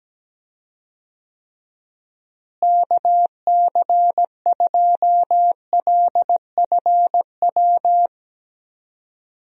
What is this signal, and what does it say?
Morse code, heard as a keyed tone, about 17 wpm, 700 Hz. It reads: KC2LFW
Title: USA Call Signs - 6 Characters - Repeated 2x - 17wpm